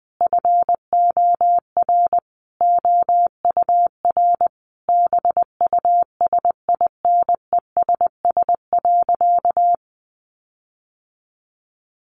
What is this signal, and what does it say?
Morse code, heard as a keyed tone, about 20 wpm, 700 Hz. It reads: FOR OUR BUSINESS.
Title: Sentences from Top 1400 Words 20wpm